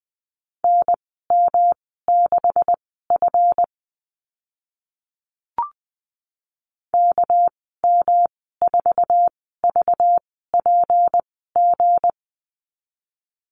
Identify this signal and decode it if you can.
Morse code, heard as a keyed tone, about 20 wpm, spaced as for 15 wpm, 700 Hz. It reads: NM6F KM4VPG
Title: 200 Lesson 32g - Letter k - Call Signs 20wpm